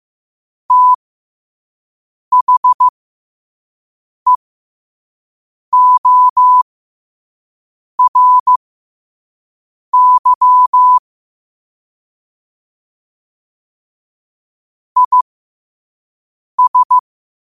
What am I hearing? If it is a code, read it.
Morse code, heard as a keyed tone, about 15 wpm, spaced as for 5 wpm, 1000 Hz. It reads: THEORY IS
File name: qotdfiles / 5wpm